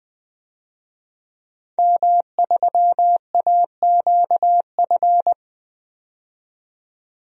Morse code, heard as a keyed tone, about 21 wpm, 700 Hz. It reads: M3AQF
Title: DXCC-100 Call Signs - 5 Characters - 20wpm